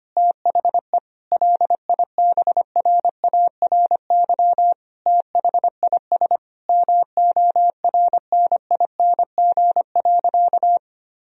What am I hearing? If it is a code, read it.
Morse code, heard as a keyed tone, about 25 wpm, 700 Hz. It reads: THE LIBRARY THIS MORNING.
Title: Sentences from Top 1000 Words - Review Part 4 - 25wpm